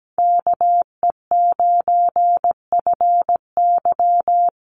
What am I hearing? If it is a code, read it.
Morse code, heard as a keyed tone, about 17 wpm, 700 Hz. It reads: KE9FY
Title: USA Call Signs Repeated 2x 17wpm